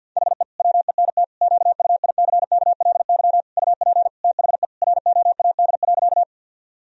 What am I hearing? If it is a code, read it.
Morse code, heard as a keyed tone, about 52 wpm, 700 Hz. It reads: HE WENT QUICKLY UP THE ROAD.
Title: Sentences from Top 1000 Words - Review Part 2 - 50wpm